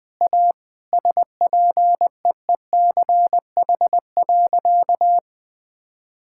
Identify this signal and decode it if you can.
Morse code, heard as a keyed tone, about 21 wpm, 700 Hz. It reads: A SPEECH.
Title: Sentences with 5-Word Phrases - Spoken Beginning - 20wpm